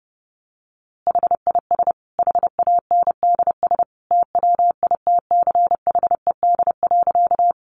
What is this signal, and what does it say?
Morse code, heard as a keyed tone, about 28 wpm, 700 Hz. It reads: HIS HANDS TWITCHED.
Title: A Sound Of Thunder 30wpm